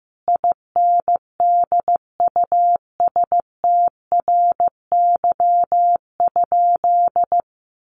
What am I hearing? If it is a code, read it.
Morse code, heard as a keyed tone, about 14 wpm, 700 Hz. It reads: INDUSTRY?
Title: Sentences from Top 1300 Words 15wpm